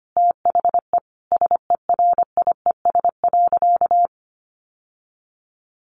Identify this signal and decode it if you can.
Morse code, heard as a keyed tone, about 25 wpm, 700 Hz. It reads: THE SERIES.
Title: Sentences from Top 1200 Words 25wpm